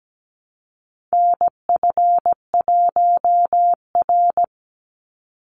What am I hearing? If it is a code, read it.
Morse code, heard as a keyed tone, about 17 wpm, 700 Hz. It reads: NF1R
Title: NA SOTA Chaser Callsigns 17wpm